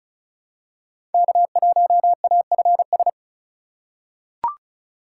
Morse code, heard as a keyed tone, about 36 wpm, 700 Hz. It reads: K1AFS
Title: USA Call Signs - 5 Characters - 4x Character Spacing - 35wpm